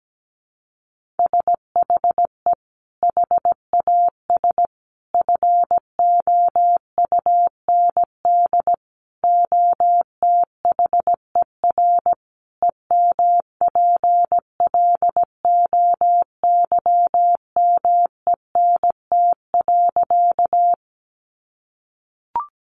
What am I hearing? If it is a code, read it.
Morse code, heard as a keyed tone, about 17 wpm, 700 Hz. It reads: SHE HAS FOUND OTHER EMPLOYMENT.
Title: Sentences from Top 1500 Words 17wpm